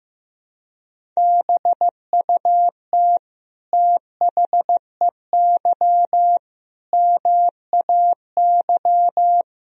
Morse code, heard as a keyed tone, about 15 wpm, 700 Hz. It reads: BUT THEY MAY
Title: Sentences from Top 300 Words and Abbreviations 15wpm